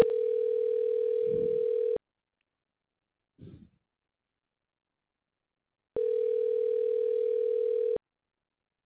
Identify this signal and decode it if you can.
Ringback tone (North America), 440 + 480 Hz.